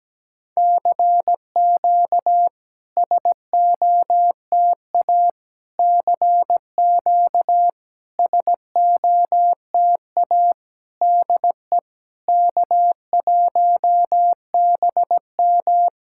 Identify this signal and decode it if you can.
Morse code, heard as a keyed tone, about 16 wpm, 700 Hz. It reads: CQ SOTA CQ SOTA DE K1BM
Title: SOTA QSO Practice 17wpm